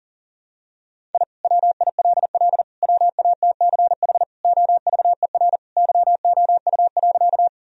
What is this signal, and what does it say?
Morse code, heard as a keyed tone, about 40 wpm, 700 Hz. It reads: I WILL WATCH OVER YOU.
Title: Sentences from Top 1000 Words - Review Part 1 - 40wpm